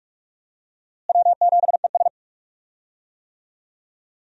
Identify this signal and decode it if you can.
Morse code, heard as a keyed tone, about 46 wpm, 700 Hz. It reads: W7ES
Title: USA Call Signs - 4 Characters - Repeated 2x - 45wpm